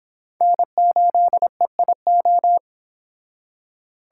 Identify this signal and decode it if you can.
Morse code, heard as a keyed tone, about 26 wpm, 700 Hz. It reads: N8EIO